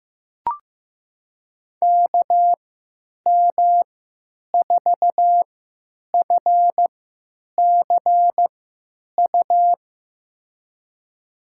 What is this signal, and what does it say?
Morse code, heard as a keyed tone, about 15 wpm, spaced as for 9 wpm, 700 Hz. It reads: KM4FCU
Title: USA Call Signs - 6 Characters - 3x Character Spacing - 15wpm